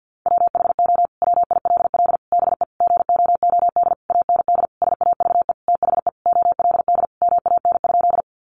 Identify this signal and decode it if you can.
Morse code, heard as a keyed tone, about 50 wpm, 700 Hz. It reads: WHO WILL BE GOOD AND SAVE THE OLD MAN?
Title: Sentences from Top 1100 Words 50wpm